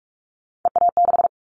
Morse code, heard as a keyed tone, about 46 wpm, 700 Hz. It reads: EA6
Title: Callsign Prefixes for Top 100 DXCC - 45wpm